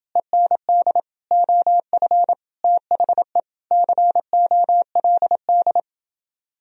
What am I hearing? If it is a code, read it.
Morse code, heard as a keyed tone, about 29 wpm, 700 Hz. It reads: END OF THE COLD